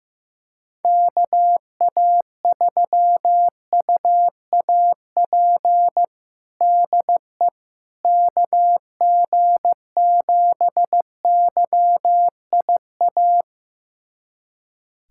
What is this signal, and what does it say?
Morse code, heard as a keyed tone, about 15 wpm, 700 Hz. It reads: KA3UAP DE KG7YIA
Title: USA Callsign DE Callsign - 6 Characters - 15wpm